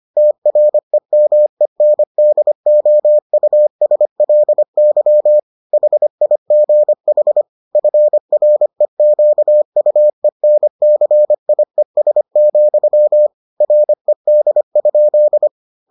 Morse code, heard as a keyed tone, about 25 wpm, 600 Hz. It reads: TREMENDOUSLY HIGH FREQUENCIES, RED?